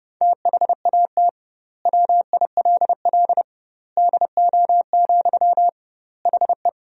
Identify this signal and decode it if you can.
Morse code, heard as a keyed tone, about 30 wpm, 700 Hz. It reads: THAT WILL DO, HE
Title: 2x Word-Spacing - Sentences from Top 100 Words - 30wpm